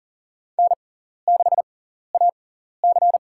Morse code, heard as a keyed tone, about 40 wpm, spaced as for 14 wpm, 700 Hz. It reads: N6AC
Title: USA Call Signs - 4 Characters - 6x Character Spacing - 40wpm